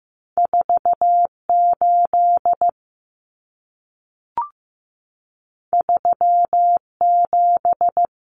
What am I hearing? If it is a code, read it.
Morse code, heard as a keyed tone, about 15 wpm, 700 Hz. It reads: 48 37
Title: Numbers - 2-Digit 15wpm